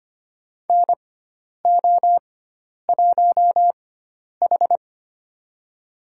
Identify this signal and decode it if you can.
Morse code, heard as a keyed tone, about 25 wpm, spaced as for 10 wpm, 700 Hz. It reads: NO1H
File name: Lesson 16g - Letter c - Call Signs 25wpm